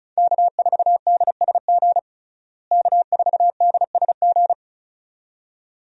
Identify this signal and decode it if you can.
Morse code, heard as a keyed tone, about 34 wpm, 700 Hz. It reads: K4DSG K4DSG
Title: USA Call Signs - 5 Characters - Repeated 2x - 35wpm